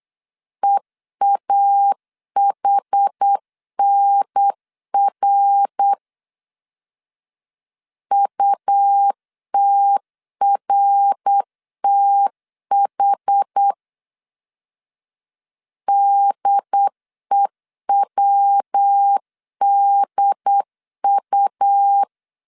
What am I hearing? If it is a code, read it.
Morse code, heard as a keyed tone, about 8 wpm, 800 Hz. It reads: EAHNR UTRTH DEWDU